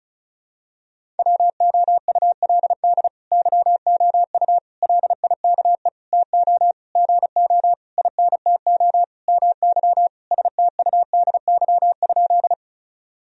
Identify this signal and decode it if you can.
Morse code, heard as a keyed tone, about 36 wpm, 700 Hz. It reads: WOULD YOU LIKE TO GO INTO MY STUDY?
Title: Sentences from Top 1000 Words - Review Part 1 - 35wpm